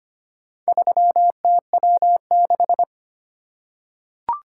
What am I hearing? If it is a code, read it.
Morse code, heard as a keyed tone, about 25 wpm, 700 Hz. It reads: 3TW6
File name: practice / 25wpm / Four Letter-Number Combinations - 25wpm